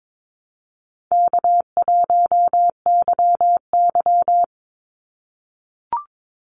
Morse code, heard as a keyed tone, about 22 wpm, 700 Hz. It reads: K1YY
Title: Call Signs - 22wpm